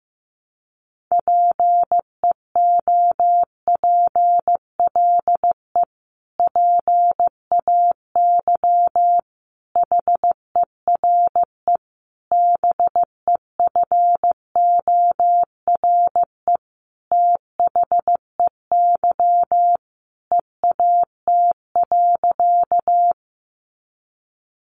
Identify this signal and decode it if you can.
Morse code, heard as a keyed tone, about 15 wpm, 700 Hz. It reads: PEOPLE PAY HERE BEFORE THEY EAT.
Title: Sentences from Top 700 Words 15wpm